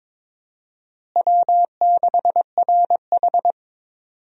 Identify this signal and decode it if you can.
Morse code, heard as a keyed tone, about 22 wpm, 700 Hz. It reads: W6RH
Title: USA Call Signs - 4 Characters - Repeated 3x - 22wpm